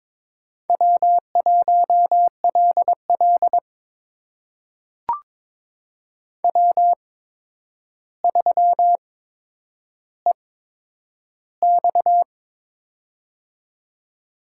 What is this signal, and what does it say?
Morse code, heard as a keyed tone, about 22 wpm, spaced as for 9 wpm, 700 Hz. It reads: W1LL W3EX
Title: USA Call Signs - 4 Characters - 8x Character Spacing - 22wpm